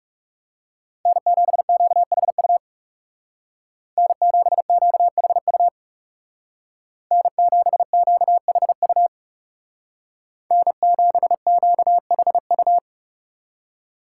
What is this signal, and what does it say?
Morse code, heard as a keyed tone, about 36 wpm, 700 Hz. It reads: N7QHU N7QHU N7QHU N7QHU
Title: Speed Racing - USA 1x3 Call Signs 30wpm